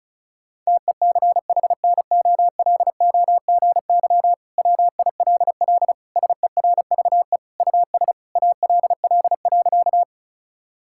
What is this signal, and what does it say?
Morse code, heard as a keyed tone, about 36 wpm, 700 Hz. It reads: TECHNOLOGY WILL SERVE US ALL.
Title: Sentences from Top 1000 Words - Review Part 5 - 35wpm